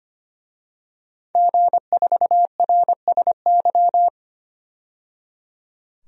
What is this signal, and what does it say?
Morse code, heard as a keyed tone, about 25 wpm, 700 Hz. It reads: G4RSY
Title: DXCC-100 Call Signs - 5 Characters - 25wpm